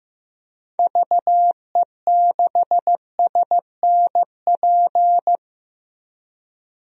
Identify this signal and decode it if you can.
Morse code, heard as a keyed tone, about 15 wpm, 700 Hz. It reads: VE6SNP
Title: DXCC-100 Call Signs - 6 Characters - Repeated 2x - 15wpm